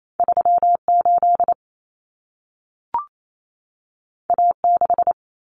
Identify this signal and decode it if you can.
Morse code, heard as a keyed tone, about 28 wpm, 700 Hz. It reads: 38 A6
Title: Common Cut Numbers - 2-Digit 28wpm